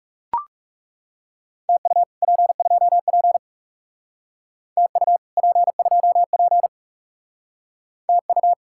Morse code, heard as a keyed tone, about 38 wpm, 700 Hz. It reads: TU P2P TU P2P TU